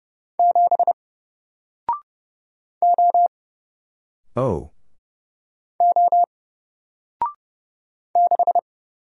Morse code, heard as a keyed tone, about 30 wpm, 700 Hz. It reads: 7 O O 6
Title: Single Letter-Number 30wpm